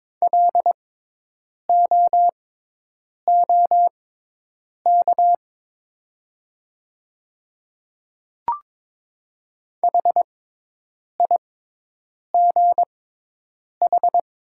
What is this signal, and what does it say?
Morse code, heard as a keyed tone, about 22 wpm, spaced as for 8 wpm, 700 Hz. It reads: LOOK HIGH
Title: 6x Character Spacing - Top 200 Words - 22wpm